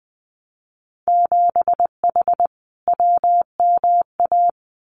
Morse code, heard as a keyed tone, about 19 wpm, 700 Hz. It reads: 7H WMA
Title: Winter Field Day Exchanges Encore 20wpm